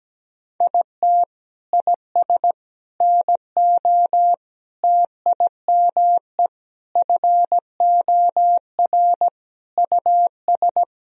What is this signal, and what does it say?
Morse code, heard as a keyed tone, about 17 wpm, 700 Hz. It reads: IT IS NO TIME FOR US
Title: Sentences from Top 1000 Words - Review Part 3 - 17wpm